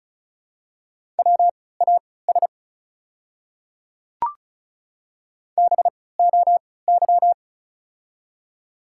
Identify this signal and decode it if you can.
Morse code, heard as a keyed tone, about 34 wpm, spaced as for 20 wpm, 700 Hz. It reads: WAS BOY
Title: 3x Character Spacing - Top 300 Words - 35wpm